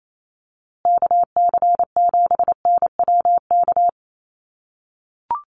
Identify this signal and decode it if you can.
Morse code, heard as a keyed tone, about 28 wpm, 700 Hz. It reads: KC7NWK
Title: USA Call Signs Repeated 3x 28wpm